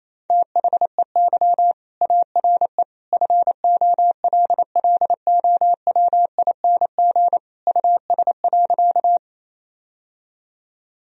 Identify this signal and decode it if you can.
Morse code, heard as a keyed tone, about 27 wpm, 700 Hz. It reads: THEY ARE FOLLOWING US.
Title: Sentences from Top 1000 Words - Review Part 4 - 28wpm